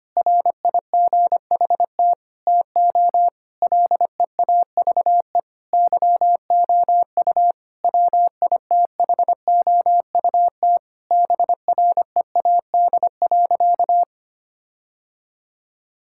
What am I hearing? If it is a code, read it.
Morse code, heard as a keyed tone, about 25 wpm, 700 Hz. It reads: RIGHT TO LEAVE YOU WITHOUT BREAD.
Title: Sentences from Top 1600 Words 25wpm